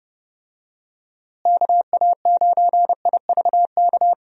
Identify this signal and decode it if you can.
Morse code, heard as a keyed tone, about 31 wpm, 700 Hz. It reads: KA9IVK
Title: USA Call Signs - 6 Characters - Repeated 2x - 30wpm